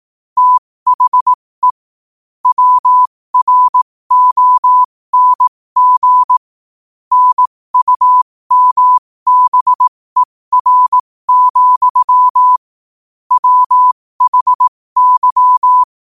Morse code, heard as a keyed tone, about 18 wpm, spaced as for 15 wpm, 1000 Hz. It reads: THE WRONG NUMBER, WHY
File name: qotdfiles / 15wpm